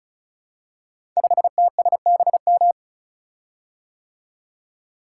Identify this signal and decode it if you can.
Morse code, heard as a keyed tone, about 36 wpm, 700 Hz. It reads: 5TSBM
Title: Code Groups - 5 Letter:Numbers - 35wpm